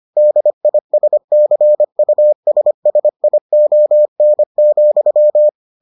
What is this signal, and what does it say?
Morse code, heard as a keyed tone, about 25 wpm, 600 Hz. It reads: DISCUSSION,